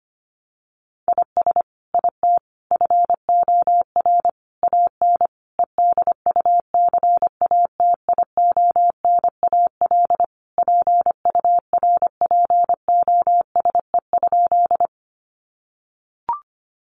Morse code, heard as a keyed tone, about 25 wpm, 700 Hz. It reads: IS IT FOR AN EDUCATIONAL PURPOSE?
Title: Sentences from Top 1200 Words 25wpm